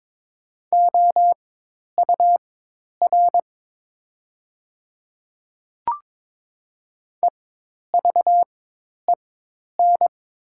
Morse code, heard as a keyed tone, about 22 wpm, spaced as for 10 wpm, 700 Hz. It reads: OUR EVEN
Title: 4x Character Spacing - Top 200 Words - 22wpm